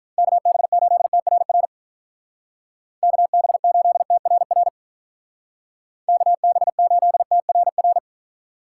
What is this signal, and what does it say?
Morse code, heard as a keyed tone, about 44 wpm, 700 Hz. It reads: KB8TRR KB8TRR KB8TRR